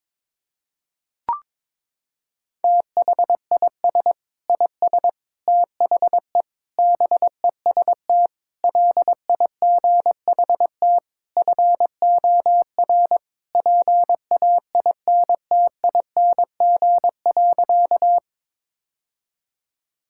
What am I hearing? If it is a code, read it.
Morse code, heard as a keyed tone, about 22 wpm, 700 Hz. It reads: THIS IS THE BEST LIGHT FOR PAINTING.
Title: Sentences from Top 1000 Words - Review Part 6 - 22wpm